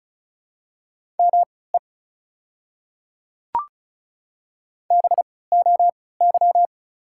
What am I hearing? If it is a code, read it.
Morse code, heard as a keyed tone, about 36 wpm, spaced as for 20 wpm, 700 Hz. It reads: ME BOY
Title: 3x Character Spacing - Top 200 Words - 35wpm